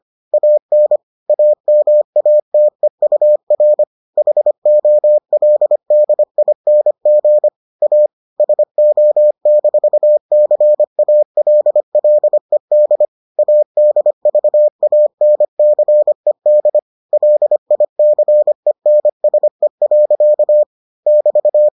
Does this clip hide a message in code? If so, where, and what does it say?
Yes: Morse code from 0.3 to 21.7 s, heard as a keyed tone, about 25 wpm, 600 Hz. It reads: AN AMATEUR HOLDING A SO-CALLED ADVANCED LICENSE. <BT>